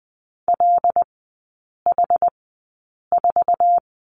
Morse code, heard as a keyed tone, about 19 wpm, spaced as for 8 wpm, 700 Hz. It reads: LH4
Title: 074 Lesson 13e - Letter L - Three Characters 20wpm